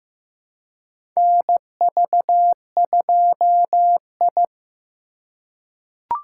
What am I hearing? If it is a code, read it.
Morse code, heard as a keyed tone, about 15 wpm, 700 Hz. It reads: NV2I
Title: USA Call Signs - 3x Character Spacing - 15wpm